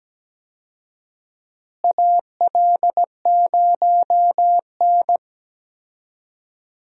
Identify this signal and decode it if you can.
Morse code, heard as a keyed tone, about 17 wpm, 700 Hz. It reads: AL0N
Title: USA Call Signs - 4 Characters - Repeated 3x - 17wpm